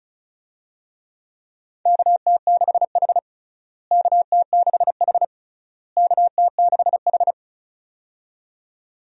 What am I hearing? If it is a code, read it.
Morse code, heard as a keyed tone, about 36 wpm, 700 Hz. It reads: KT6H KT6H KT6H
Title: USA Call Signs - 4 Characters - Repeated 3x - 35wpm